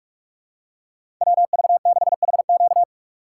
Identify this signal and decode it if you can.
Morse code, heard as a keyed tone, about 46 wpm, 700 Hz. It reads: WV6HQ
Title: USA Call Signs - 5 Characters - 3x Character Spacing - 45wpm